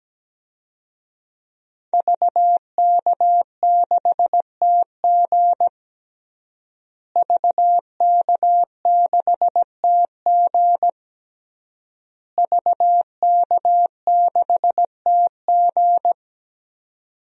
Morse code, heard as a keyed tone, about 17 wpm, 700 Hz. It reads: VK6TG VK6TG VK6TG
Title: DXCC-100 Call Signs - 5 Characters - Repeated 3x - 17wpm